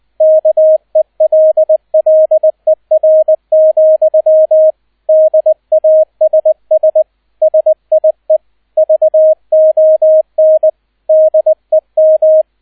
Morse code, heard as a keyed tone, about 20 wpm, 630 Hz, with radio noise. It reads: KELLER, DASS SIE VON DEM